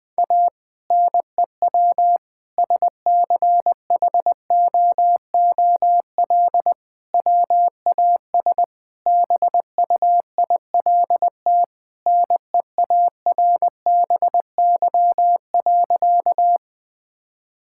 Morse code, heard as a keyed tone, about 21 wpm, 700 Hz. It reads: A NEW SCHOOL WAS BUILT NEARBY.
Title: Sentences with 2-Word Phrases - 20wpm